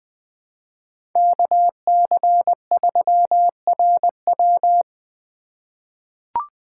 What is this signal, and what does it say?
Morse code, heard as a keyed tone, about 21 wpm, 700 Hz. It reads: KC3RW
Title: 2023 POTA Activator Call Signs - 20wpm